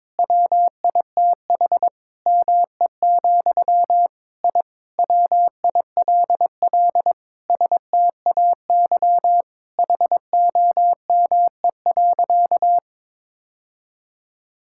Morse code, heard as a keyed tone, about 22 wpm, 700 Hz. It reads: WITH ME, I WILL STAY HOME.
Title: Sentences from Top 1000 Words - Review Part 6 - 22wpm